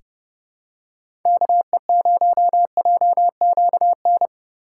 Morse code, heard as a keyed tone, about 30 wpm, 700 Hz. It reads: KE0JQN
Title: USA Call Signs - 6 Characters - 6x Character Spacing - 30wpm